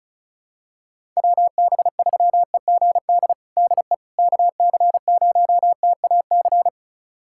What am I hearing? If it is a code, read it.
Morse code, heard as a keyed tone, about 34 wpm, 700 Hz. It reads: WB3EGD DE KC0TAC
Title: USA Callsign DE Callsign - 6 Characters - 35wpm